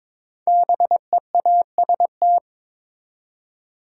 Morse code, heard as a keyed tone, about 22 wpm, 700 Hz. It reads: BEAST
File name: 5 Letter Words - First 3 Letters Given - 22wpm